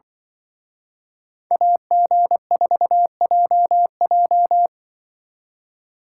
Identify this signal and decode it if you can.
Morse code, heard as a keyed tone, about 24 wpm, 700 Hz. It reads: AG4JJ
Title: Speed Racing - USA 2x2 Call Signs 20wpm